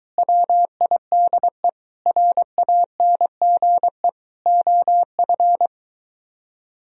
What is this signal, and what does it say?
Morse code, heard as a keyed tone, about 23 wpm, 700 Hz. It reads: WIDE RANGE OF